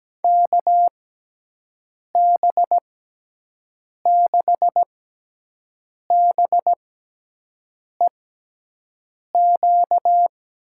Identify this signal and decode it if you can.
Morse code, heard as a keyed tone, about 17 wpm, spaced as for 6 wpm, 700 Hz. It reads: KB6BEQ
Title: USA Call Signs - 6 Characters - 6x Character Spacing - 17wpm